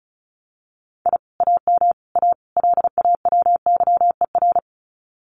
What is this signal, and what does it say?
Morse code, heard as a keyed tone, about 36 wpm, 700 Hz. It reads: I AM A LAWYER
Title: Common QSO Phrases Encore 35wpm